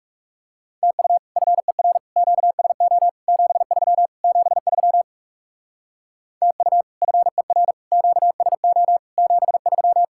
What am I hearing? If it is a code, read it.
Morse code, heard as a keyed tone, about 39 wpm, 700 Hz. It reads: TU FER QSO 73 73 TU FER QSO 73